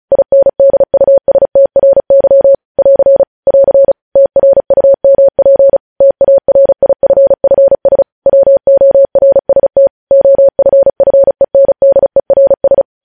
Morse code, heard as a keyed tone, about 36 wpm, 550 Hz. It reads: INDUSTRY <AR> <AR> TRUMP TARIFFS WORST OFFENDERS
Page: news headlines in morse code at 35 wpm